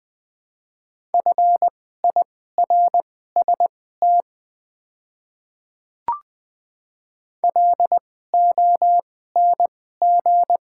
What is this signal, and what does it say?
Morse code, heard as a keyed tone, about 20 wpm, spaced as for 15 wpm, 700 Hz. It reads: FIRST LONG
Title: 2x Character Spacing - Top 100 Words - 20wpm